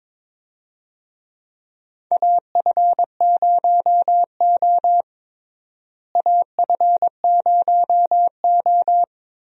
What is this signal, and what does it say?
Morse code, heard as a keyed tone, about 22 wpm, 700 Hz. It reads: AF0O AF0O
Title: USA Call Signs - 4 Characters - Repeated 3x - 22wpm